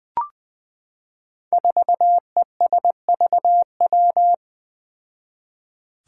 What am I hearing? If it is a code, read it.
Morse code, heard as a keyed tone, about 20 wpm, 700 Hz. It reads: 4ESVW
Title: Five Challenging Letter-Number Combinations 20wpm